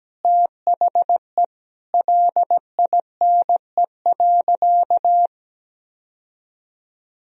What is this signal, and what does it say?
Morse code, heard as a keyed tone, about 17 wpm, 700 Hz. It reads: THE LINE.
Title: Sentences from Top 1000 Words - Review Part 1 - 17wpm